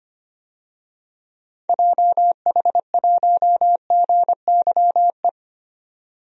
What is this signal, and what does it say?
Morse code, heard as a keyed tone, about 25 wpm, 700 Hz. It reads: JH1GYE
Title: DXCC-100 Call Signs - 6 Characters - 25wpm